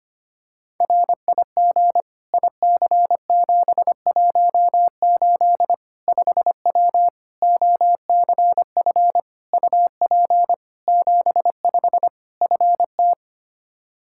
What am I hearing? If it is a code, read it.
Morse code, heard as a keyed tone, about 25 wpm, 700 Hz. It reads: RIG IC718 5W OCF UP 75 FT
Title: K1USN SST QSO Practice 25wpm